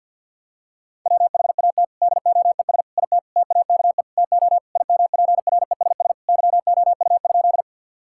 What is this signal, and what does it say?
Morse code, heard as a keyed tone, about 50 wpm, 700 Hz. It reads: WHAT DOES IT TAKE TO IMPRESS YOU?
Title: Sentences from Top 1600 Words 50wpm